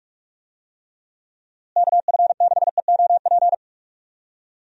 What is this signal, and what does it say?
Morse code, heard as a keyed tone, about 45 wpm, 700 Hz. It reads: KF6EOP
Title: USA Call Signs - 6 Characters - Repeated 3x - 45wpm